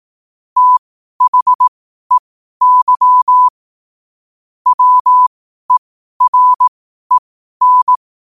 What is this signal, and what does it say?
Morse code, heard as a keyed tone, about 18 wpm, spaced as for 12 wpm, 1000 Hz. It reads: THEY WEREN
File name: qotdfiles / 12wpm